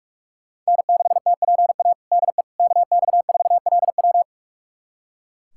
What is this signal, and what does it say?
Morse code, heard as a keyed tone, about 42 wpm, 700 Hz. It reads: N6TPA DE KX4LW
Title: USA Callsign DE Callsign - 5 Characters - 45wpm